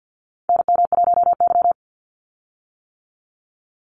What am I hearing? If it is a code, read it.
Morse code, heard as a keyed tone, about 50 wpm, 700 Hz. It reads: NM1Y
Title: Call Signs - 50wpm